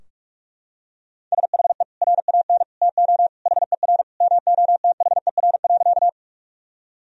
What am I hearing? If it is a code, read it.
Morse code, heard as a keyed tone, about 41 wpm, 700 Hz. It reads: SHE RAN TO HER MOTHER.
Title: Sentences from Top 1000 Words - Review Part 3 - 45wpm